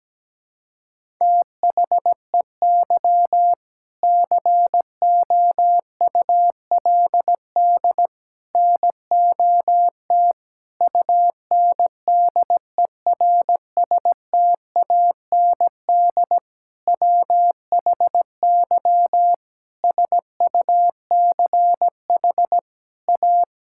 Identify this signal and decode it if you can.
Morse code, heard as a keyed tone, about 17 wpm, 700 Hz. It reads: THEY COULD NOT UNDERSTAND WHY SUCH A
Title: Sentences from Top 1000 Words - Review Part 6 - 17wpm